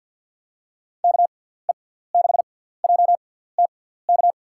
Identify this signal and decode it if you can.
Morse code, heard as a keyed tone, about 48 wpm, spaced as for 17 wpm, 700 Hz. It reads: KE6JTK
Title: USA Call Signs - 6x Character Spacing - 50wpm